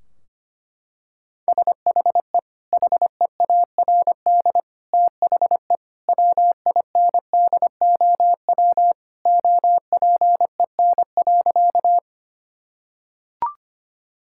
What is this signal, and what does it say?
Morse code, heard as a keyed tone, about 25 wpm, 700 Hz. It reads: SHE HEARD THE WINDOW OPEN.
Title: Sentences from Top 1600 Words 25wpm